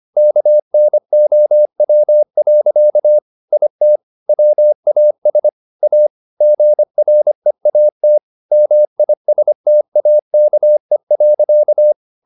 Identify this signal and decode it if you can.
Morse code, heard as a keyed tone, about 25 wpm, 600 Hz. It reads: KNOW. IT WAS A GREAT MISTAKE.
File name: the old man and the sea part 1 25wpm